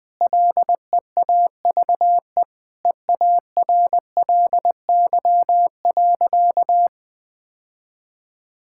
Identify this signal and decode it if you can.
Morse code, heard as a keyed tone, about 19 wpm, 700 Hz. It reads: LEAVE EARLY.
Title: Sentences with 5-Word Phrases - 20wpm